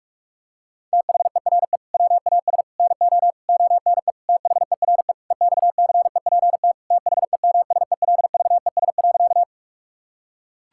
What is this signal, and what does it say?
Morse code, heard as a keyed tone, about 43 wpm, 700 Hz. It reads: THERE WAS NO ONE THERE EXCEPT THEMSELVES.
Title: Sentences from Top 1600 Words 45wpm